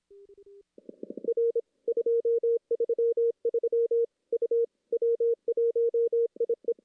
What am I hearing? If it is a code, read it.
Morse code, heard as a keyed tone, about 26 wpm, 470 Hz. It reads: R 233 U W1IE
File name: contests / CW-SS-k5tr- -1600